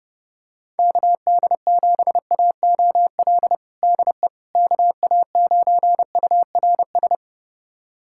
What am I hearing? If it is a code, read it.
Morse code, heard as a keyed tone, about 32 wpm, 700 Hz. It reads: KD7AOL DE KA9URS
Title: USA Callsign DE Callsign 30wpm